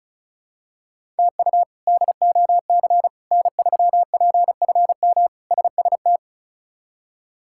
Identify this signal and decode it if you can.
Morse code, heard as a keyed tone, about 36 wpm, 700 Hz. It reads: TU DOC N3PFM SST
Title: K1USN SST QSO Practice - 35wpm